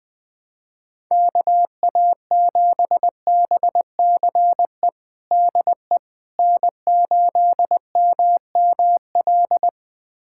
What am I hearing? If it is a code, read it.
Morse code, heard as a keyed tone, about 20 wpm, 700 Hz. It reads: KA7BCE DE N8MML
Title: USA Callsign DE Callsign 20wpm